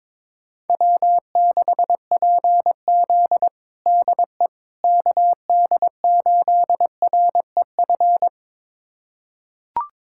Morse code, heard as a keyed tone, about 21 wpm, 700 Hz. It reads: W6PZ DE KD8REF
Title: USA Callsign DE Callsign 22wpm